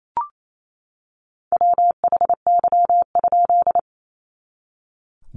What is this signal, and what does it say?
Morse code, heard as a keyed tone, about 28 wpm, 700 Hz. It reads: WHY?
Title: A Sound Of Thunder 28wpm